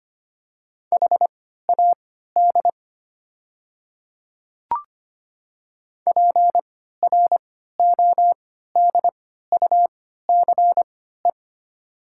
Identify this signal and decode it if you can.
Morse code, heard as a keyed tone, about 25 wpm, spaced as for 14 wpm, 700 Hz. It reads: HAD PRODUCE
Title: 3x Character Spacing - Top 500 Words - 25wpm